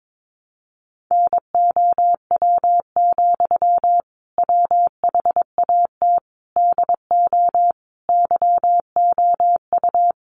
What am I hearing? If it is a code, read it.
Morse code, heard as a keyed tone, about 22 wpm, 700 Hz. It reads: NOW, WHAT DO YOU
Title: Sentences from Top 200 Words 22wpm